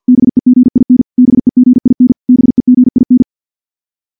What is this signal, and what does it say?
Morse code, heard as a keyed tone, about 50 wpm, 250 Hz. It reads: BEGIN BEGIN BEGIN
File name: morse / morse-training-50WPM